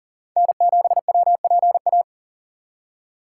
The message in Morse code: N7WPA

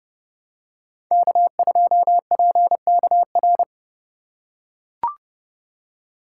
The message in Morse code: K2PKR

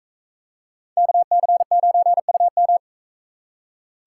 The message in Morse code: KC9UM

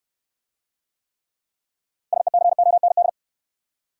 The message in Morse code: SECOND

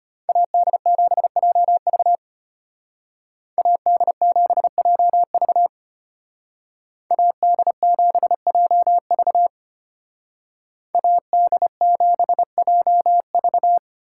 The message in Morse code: AD7JV AD7JV AD7JV AD7JV

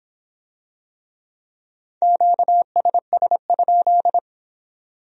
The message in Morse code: QSS?

